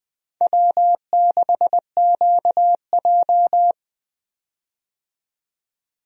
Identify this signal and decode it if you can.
Morse code, heard as a keyed tone, about 21 wpm, 700 Hz. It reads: W6QJ
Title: USA Call Signs - 4 Characters - 20wpm